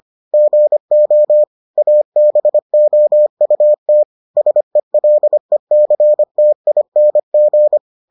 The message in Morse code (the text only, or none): GO ABOUT SELECTING